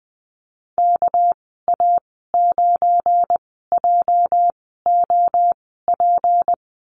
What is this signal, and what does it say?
Morse code, heard as a keyed tone, about 20 wpm, spaced as for 15 wpm, 700 Hz. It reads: KA9JOP
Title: 219 Lesson 35g - Number 0 - Call Signs 20wpm